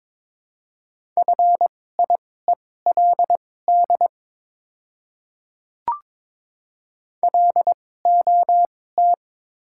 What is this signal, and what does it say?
Morse code, heard as a keyed tone, about 22 wpm, spaced as for 16 wpm, 690 Hz. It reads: FIELD LOT
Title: 2x Character Spacing - Top 500 Words - 22wpm